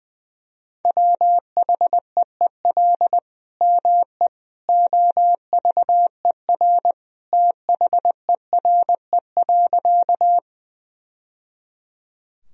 Morse code, heard as a keyed tone, about 20 wpm, 700 Hz. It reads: WHEEL ME OVER THERE.
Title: Sentences from Top 1000 Words - Review Part 3 - 20wpm